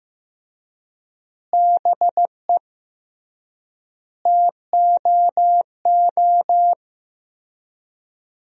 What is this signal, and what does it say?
Morse code, heard as a keyed tone, about 15 wpm, 700 Hz. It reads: BE TOO
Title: 3x Word-Spacing - Sentences from Top 300 Words - 15wpm